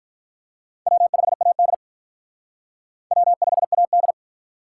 Morse code, heard as a keyed tone, about 43 wpm, 700 Hz. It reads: W5AD W5AD
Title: Speed Racing - USA Call Signs 35wpm